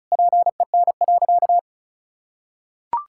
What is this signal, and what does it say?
Morse code, heard as a keyed tone, about 36 wpm, 700 Hz. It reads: PEN.
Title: Pangram Challenge v2 - 35wpm